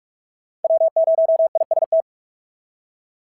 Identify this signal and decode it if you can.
Morse code, heard as a keyed tone, about 48 wpm, 650 Hz. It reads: W0IST
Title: SOTA QSO Practice 45wpm